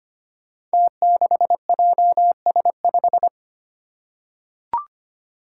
Morse code, heard as a keyed tone, about 25 wpm, 700 Hz. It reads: T6JS5
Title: Five Challenging Letter-Number Combinations 25wpm